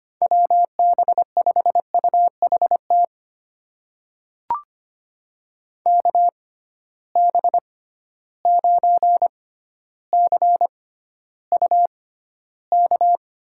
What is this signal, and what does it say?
Morse code, heard as a keyed tone, about 25 wpm, spaced as for 13 wpm, 700 Hz. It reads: WB5UHT KB9CUK
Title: USA Call Signs - 6 Characters - 6x Character Spacing - 25wpm